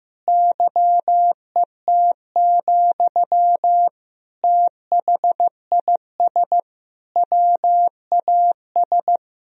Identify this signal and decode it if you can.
Morse code, heard as a keyed tone, about 15 wpm, 700 Hz. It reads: YET, THIS WAS
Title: Sentences from Top 1000 Words - Review Part 3 - 15wpm